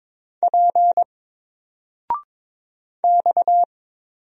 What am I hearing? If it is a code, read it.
Morse code, heard as a keyed tone, about 22 wpm, 700 Hz. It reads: P X
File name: Single-Letter-Number-22wpm